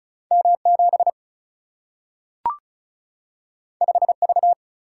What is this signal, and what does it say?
Morse code, heard as a keyed tone, about 36 wpm, 700 Hz. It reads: M 7 5 V
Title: Single-Letter Single-Number 35wpm